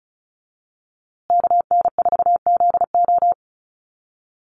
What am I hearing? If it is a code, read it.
Morse code, heard as a keyed tone, about 34 wpm, 700 Hz. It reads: KN4ZO
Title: USA Call Signs Repeated 2x 35wpm